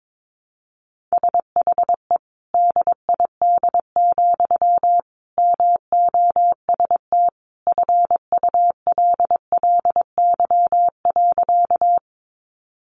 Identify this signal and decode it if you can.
Morse code, heard as a keyed tone, about 22 wpm, 700 Hz. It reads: SHE DID, MOST FULLY.
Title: Sentences from Top 1300 Words 22wpm